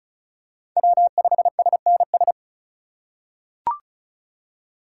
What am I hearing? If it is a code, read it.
Morse code, heard as a keyed tone, about 34 wpm, 700 Hz. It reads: W5SNS
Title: USA Call Signs - 5 Characters - 4x Character Spacing - 35wpm